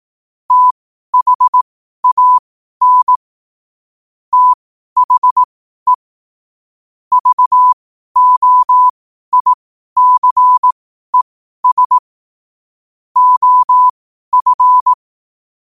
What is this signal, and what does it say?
Morse code, heard as a keyed tone, about 18 wpm, spaced as for 12 wpm, 1000 Hz. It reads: THAN THE VOICES OF
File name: qotdfiles / 12wpm